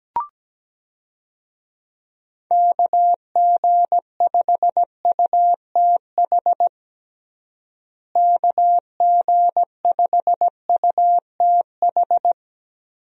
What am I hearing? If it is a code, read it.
Morse code, heard as a keyed tone, about 17 wpm, 700 Hz. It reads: KG5UTH KG5UTH
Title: USA Call Signs - 6 Characters - Repeated 3x - 17wpm